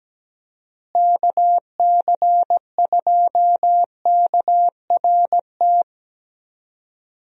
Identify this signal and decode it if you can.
Morse code, heard as a keyed tone, about 17 wpm, 700 Hz. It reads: KC2KRT
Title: USA Call Signs - 6 Characters - Repeated 2x - 17wpm